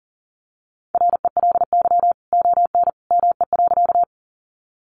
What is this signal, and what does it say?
Morse code, heard as a keyed tone, about 38 wpm, 700 Hz. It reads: RELY ON ME.
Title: Sentences from Top 1200 Words 40wpm